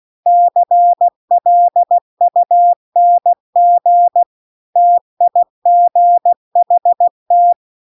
Morse code, heard as a keyed tone, about 16 wpm, 700 Hz. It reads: CLUNG TIGHT